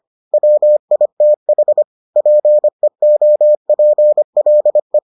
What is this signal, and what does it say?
Morse code, heard as a keyed tone, about 25 wpm, 600 Hz. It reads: WITH PEOPLE